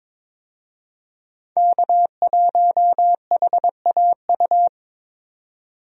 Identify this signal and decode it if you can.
Morse code, heard as a keyed tone, about 23 wpm, 700 Hz. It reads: K1HAU
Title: USA Call Signs - 5 Characters - 22wpm